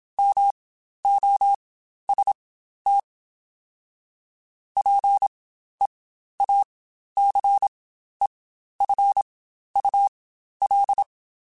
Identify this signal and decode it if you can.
Morse code, heard as a keyed tone, about 27 wpm, spaced as for 12 wpm, 800 Hz. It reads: MOST PEACEFUL
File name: Quotes Spaced / Quotes Spaced-mp3s / ChildAsleep sp @25